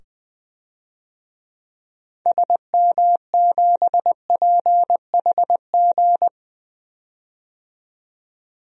Morse code, heard as a keyed tone, about 20 wpm, 700 Hz. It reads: SM7PHG
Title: DXCC-100 Call Signs - 6 Characters - 20wpm